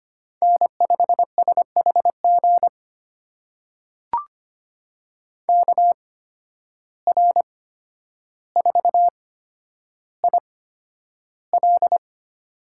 Morse code, heard as a keyed tone, about 25 wpm, spaced as for 11 wpm, 700 Hz. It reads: N5SHG KR4IL
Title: USA Call Signs - 5 Characters - 8x Character Spacing - 25wpm